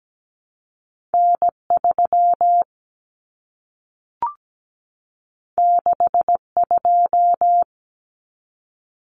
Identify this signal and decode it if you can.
Morse code, heard as a keyed tone, about 17 wpm, 700 Hz. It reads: N 3 6 2
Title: Common Cut Numbers - 2-Digit 17wpm